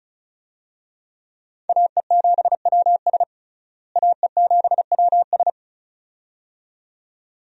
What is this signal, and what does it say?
Morse code, heard as a keyed tone, about 34 wpm, 700 Hz. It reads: AE7WS AE7WS
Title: USA Call Signs - 5 Characters - Repeated 2x - 35wpm